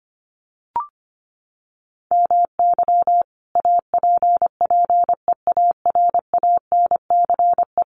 Morse code, heard as a keyed tone, about 25 wpm, 700 Hz. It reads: MY APPEARANCE
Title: Sentences from Top 1100 Words 25wpm